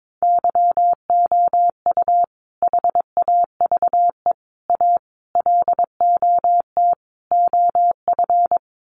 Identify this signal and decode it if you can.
Morse code, heard as a keyed tone, about 22 wpm, 700 Hz. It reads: YOU HAVE A LOT OF
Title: Sets of 5 Words Encore 22wpm